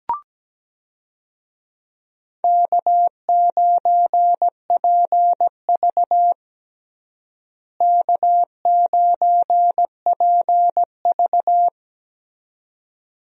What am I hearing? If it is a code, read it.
Morse code, heard as a keyed tone, about 17 wpm, 700 Hz. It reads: K9PV K9PV
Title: USA Call Signs - 4 Characters - Repeated 2x - 17wpm